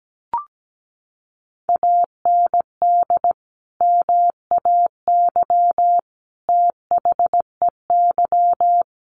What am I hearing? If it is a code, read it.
Morse code, heard as a keyed tone, about 17 wpm, 700 Hz. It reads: AND MAY THEY
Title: Sentences from Top 1500 Words 17wpm